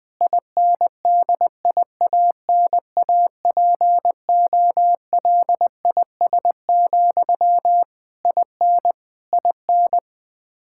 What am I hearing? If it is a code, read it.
Morse code, heard as a keyed tone, about 21 wpm, 700 Hz. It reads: INDIANAPOLIS, IN IN
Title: K1USN SST QSO Practice 20wpm